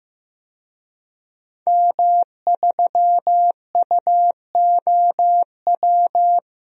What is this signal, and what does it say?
Morse code, heard as a keyed tone, about 15 wpm, 700 Hz. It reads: M3UOW
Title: DXCC-100 Call Signs - 5 Characters - 15wpm